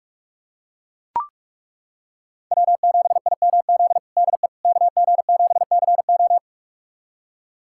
Morse code, heard as a keyed tone, about 41 wpm, 700 Hz. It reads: W7IMZ DE KG7CO
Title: USA Callsign DE Callsign - 5 Characters - 45wpm